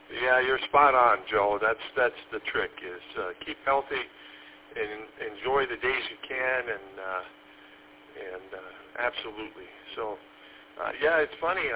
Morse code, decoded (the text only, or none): none